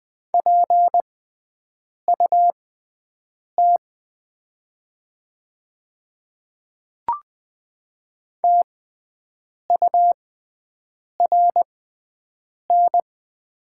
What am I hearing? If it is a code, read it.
Morse code, heard as a keyed tone, about 20 wpm, spaced as for 7 wpm, 700 Hz. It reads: PUT TURN
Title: 6x Character Spacing - Top 200 Words - 20wpm